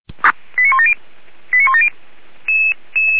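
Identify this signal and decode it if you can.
Five-tone selective call (ZVEI), heard as quick runs of five tones, tones of about 70 ms. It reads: 89170 89170
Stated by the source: ZVEI five-tone selective call received on an SDR.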